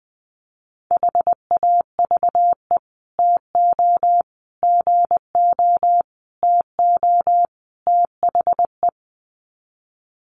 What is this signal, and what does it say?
Morse code, heard as a keyed tone, about 20 wpm, 700 Hz. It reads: HAVE TO GO TO THE
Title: Sets of 5 Words Encore 20wpm